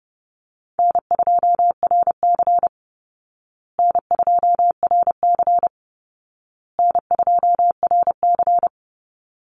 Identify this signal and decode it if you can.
Morse code, heard as a keyed tone, about 28 wpm, 700 Hz. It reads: N2RC N2RC N2RC
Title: Easy Call Signs Repeated 3x - 30wpm